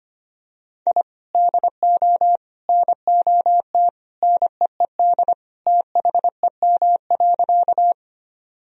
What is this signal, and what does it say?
Morse code, heard as a keyed tone, about 25 wpm, 700 Hz. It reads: I DO NOT NEED THEM.
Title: Sentences from Top 1000 Words - Review Part 1 - 25wpm